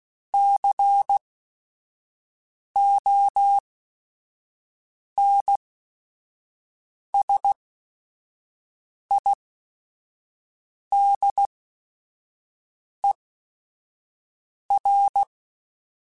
Morse code, heard as a keyed tone, about 16 wpm, spaced as for 5 wpm, 800 Hz. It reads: CONSIDER
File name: Technical Subjects / Technical Subjects-mp3s / AND Gate @5